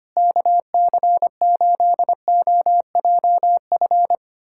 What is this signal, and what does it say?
Morse code, heard as a keyed tone, about 25 wpm, 700 Hz. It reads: KC8OJF
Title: USA Callsign DE Callsign - 6 Characters - 25wpm